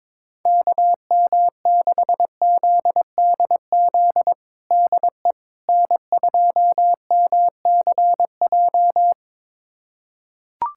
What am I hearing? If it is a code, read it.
Morse code, heard as a keyed tone, about 22 wpm, 700 Hz. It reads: KM6ZDZ DE N2MCJ
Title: USA Callsign DE Callsign 22wpm